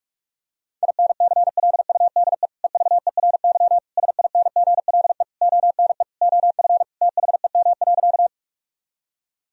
Morse code, heard as a keyed tone, about 46 wpm, 700 Hz. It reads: INCLUDE EVERY SINGLE ONE OF THEM.
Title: Sentences from Top 1000 Words - Review Part 4 - 45wpm